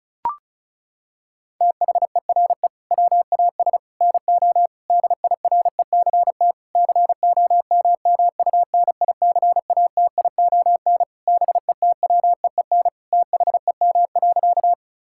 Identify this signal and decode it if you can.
Morse code, heard as a keyed tone, about 36 wpm, 700 Hz. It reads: THERE WAS NO DIRECT COMMUNICATION BETWEEN THEM.
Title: Sentences from Top 1000 Words - Review Part 6 - 35wpm